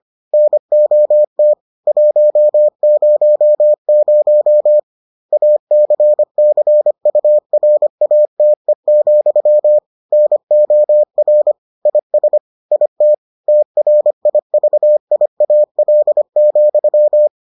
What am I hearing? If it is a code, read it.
Morse code, heard as a keyed tone, about 25 wpm, 600 Hz. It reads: NOT 100 ACCURATE, NOR IS IT TRIVIAL,